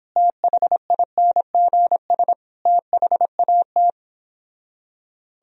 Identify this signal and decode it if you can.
Morse code, heard as a keyed tone, about 29 wpm, 700 Hz. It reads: THINGS THAT